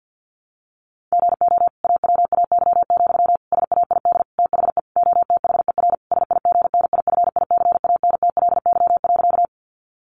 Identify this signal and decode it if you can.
Morse code, heard as a keyed tone, about 48 wpm, 700 Hz. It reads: GO AWAY, SAID THE OTHER SIGNIFICANTLY.
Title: Sentences from Top 1500 Words 50wpm